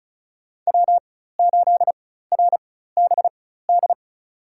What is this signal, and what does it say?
Morse code, heard as a keyed tone, about 36 wpm, spaced as for 16 wpm, 700 Hz. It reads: W8RBD
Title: USA Call Signs - 5 Characters - 4x Character Spacing - 35wpm